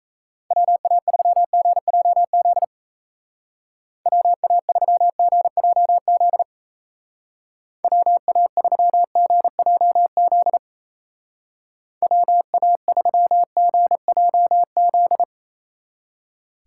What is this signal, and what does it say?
Morse code, heard as a keyed tone, about 33 wpm, 700 Hz. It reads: WA3GJZ WA3GJZ WA3GJZ WA3GJZ
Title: Speed Racing - USA Call Signs 28wpm